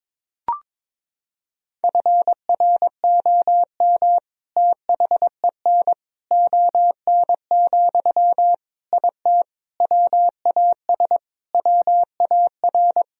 Morse code, heard as a keyed tone, about 22 wpm, 700 Hz. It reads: FROM THEN ON, IT WAS WAR
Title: Sentences from Top 1000 Words - Review Part 2 - 22wpm